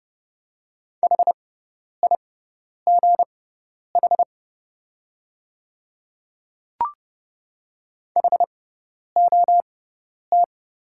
Morse code, heard as a keyed tone, about 31 wpm, spaced as for 10 wpm, 690 Hz. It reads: HIGH HOT
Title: 6x Character Spacing - Top 300 Words - 30wpm